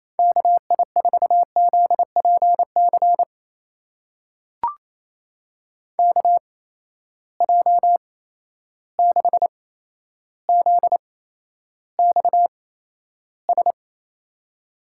Morse code, heard as a keyed tone, about 28 wpm, spaced as for 12 wpm, 700 Hz. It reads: KI4ZPC KJ6ZXS
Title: USA Call Signs - 6 Characters - 8x Character Spacing - 28wpm